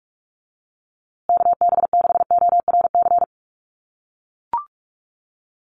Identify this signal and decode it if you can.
Morse code, heard as a keyed tone, about 44 wpm, 700 Hz. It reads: KB6ORC
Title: USA Call Signs Repeated 2x 45wpm